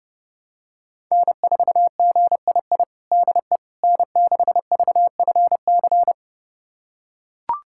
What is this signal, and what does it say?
Morse code, heard as a keyed tone, about 32 wpm, 700 Hz. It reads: N4GII DE N6VFC
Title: USA Callsign DE Callsign - 5 Characters - 30wpm